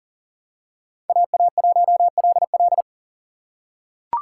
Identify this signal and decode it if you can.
Morse code, heard as a keyed tone, about 40 wpm, 700 Hz. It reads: AA1LL
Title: USA Call Signs - 5 Characters - 3x Character Spacing - 40wpm